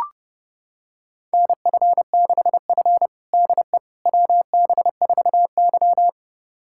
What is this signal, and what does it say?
Morse code, heard as a keyed tone, about 30 wpm, 700 Hz. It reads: NF6F DE WB4Y
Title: USA Callsign DE Callsign - 4 Characters - 30wpm